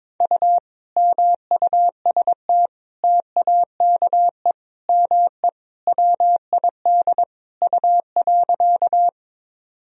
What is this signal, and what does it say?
Morse code, heard as a keyed tone, about 22 wpm, 700 Hz. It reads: U MUST TAKE ME WID U.
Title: Sentences from Top 200 Words and Abbreviations 22wpm